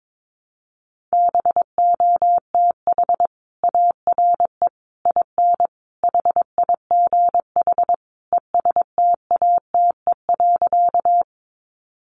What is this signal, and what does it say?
Morse code, heard as a keyed tone, about 22 wpm, 700 Hz. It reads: BOTH ARE IN HIGH ESTATE.
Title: Sentences from Top 1200 Words 22wpm